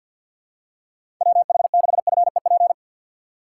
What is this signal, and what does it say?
Morse code, heard as a keyed tone, about 47 wpm, 700 Hz. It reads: WH6FEP
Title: USA Call Signs - 6 Characters - 4x Character Spacing - 50wpm